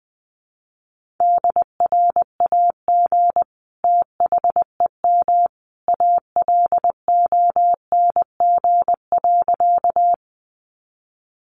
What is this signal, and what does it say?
Morse code, heard as a keyed tone, about 20 wpm, 700 Hz. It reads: DRAG THEM ALONG.
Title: Sentences from Top 1500 Words 20wpm